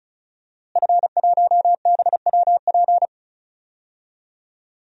Morse code, heard as a keyed tone, about 36 wpm, 700 Hz. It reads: F1BWP
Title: DXCC-100 Call Signs - 5 Characters - Repeated 2x - 35wpm